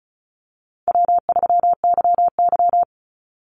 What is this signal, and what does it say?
Morse code, heard as a keyed tone, about 34 wpm, 700 Hz. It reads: W3YY
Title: Call Signs - 35wpm